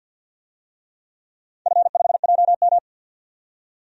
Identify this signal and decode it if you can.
Morse code, heard as a keyed tone, about 46 wpm, 700 Hz. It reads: F5JM
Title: DXCC-100 Call Signs - 4 Characters - 50wpm